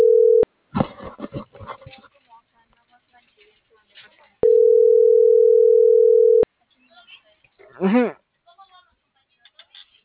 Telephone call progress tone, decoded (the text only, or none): ringback tone